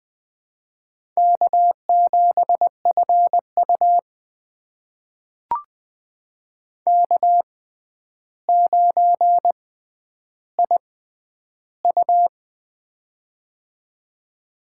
Morse code, heard as a keyed tone, about 21 wpm, spaced as for 10 wpm, 700 Hz. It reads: K7FU K9IU
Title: USA Call Signs - 4 Characters - 6x Character Spacing - 20wpm